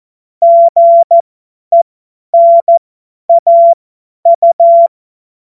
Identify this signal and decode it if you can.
Morse code, heard as a keyed tone, about 14 wpm, spaced as for 10 wpm, 680 Hz. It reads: GENAU
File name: Morsekurs / Witze / Witz 08. August 2023 50 F70